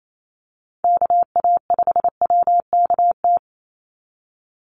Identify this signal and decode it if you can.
Morse code, heard as a keyed tone, about 28 wpm, 700 Hz. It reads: KA5WKT
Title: USA Call Signs 28wpm